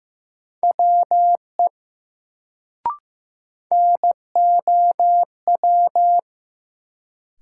Morse code, heard as a keyed tone, about 15 wpm, 700 Hz. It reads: WE NOW
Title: Sentences from Top 100 Words - New Word Drill - 15wpm